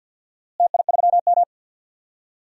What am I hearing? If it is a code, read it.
Morse code, heard as a keyed tone, about 46 wpm, 700 Hz. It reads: TI3M
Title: DXCC-100 Call Signs - 4 Characters - Repeated 2x - 50wpm